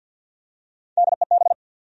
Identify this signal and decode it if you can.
Morse code, heard as a keyed tone, about 46 wpm, 700 Hz. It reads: DEB
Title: USA Common Names - Max 3 Letters - 50wpm